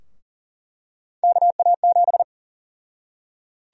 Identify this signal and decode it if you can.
Morse code, heard as a keyed tone, about 40 wpm, 700 Hz. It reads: KA7
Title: USA Call Sign Prefixes - 40wpm